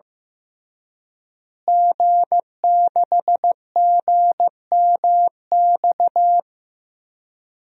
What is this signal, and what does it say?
Morse code, heard as a keyed tone, about 15 wpm, 700 Hz. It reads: G6GMX
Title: DXCC-100 Call Signs - 5 Characters - Repeated 3x - 15wpm